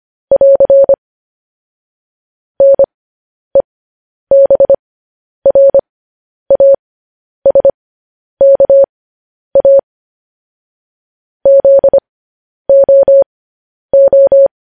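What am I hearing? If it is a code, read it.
Morse code, heard as a keyed tone, about 25 wpm, spaced as for 10 wpm, 550 Hz. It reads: <AR> NEBRASKA ZOO
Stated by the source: Morse code practice sending news headlines at 10 wpm.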